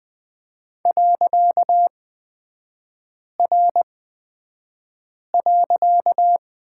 Morse code, heard as a keyed tone, about 20 wpm, spaced as for 5 wpm, 700 Hz. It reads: .R.
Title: 255 Lesson 41e - Period - Three Characters 20wpm